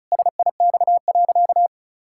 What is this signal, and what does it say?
Morse code, heard as a keyed tone, about 34 wpm, 700 Hz. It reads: SIX.
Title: Pangram Challenge v2 - 35wpm